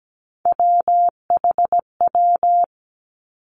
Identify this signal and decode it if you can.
Morse code, heard as a keyed tone, about 17 wpm, 700 Hz. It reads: WHW
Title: Three Challenging Letter-Number Combinations 17wpm